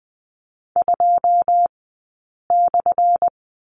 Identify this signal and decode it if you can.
Morse code, heard as a keyed tone, about 20 wpm, spaced as for 8 wpm, 700 Hz. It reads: 2/
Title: 211 Lesson 34d - Number 8 - Two Characters 20wpm